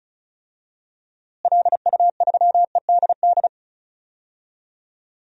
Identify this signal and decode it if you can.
Morse code, heard as a keyed tone, about 34 wpm, 700 Hz. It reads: LU3EDD
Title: DXCC-100 Call Signs - 6 Characters - Repeated 2x - 35wpm